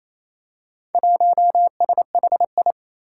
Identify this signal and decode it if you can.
Morse code, heard as a keyed tone, about 27 wpm, 700 Hz. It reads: 1SHI
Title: Four Challenging Letter-Number Combinations 28wpm